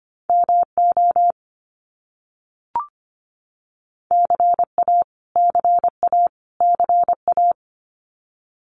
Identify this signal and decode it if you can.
Morse code, heard as a keyed tone, about 25 wpm, 700 Hz. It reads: MO CA CA CA
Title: US States Repeated 3x 25wpm